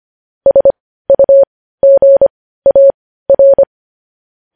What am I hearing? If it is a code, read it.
Morse code, heard as a keyed tone, about 25 wpm, spaced as for 15 wpm, 550 Hz. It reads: SUGAR
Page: News Headlines in Morse Code at 15 WPM